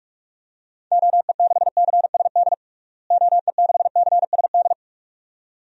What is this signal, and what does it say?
Morse code, heard as a keyed tone, about 46 wpm, 700 Hz. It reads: OE6CSD OE6CSD
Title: DXCC-100 Call Signs - 6 Characters - Repeated 2x - 45wpm